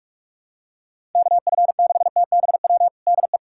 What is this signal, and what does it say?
Morse code, heard as a keyed tone, about 44 wpm, 700 Hz. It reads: KF6TBW DE
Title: USA Callsign DE Callsign - 6 Characters - 45wpm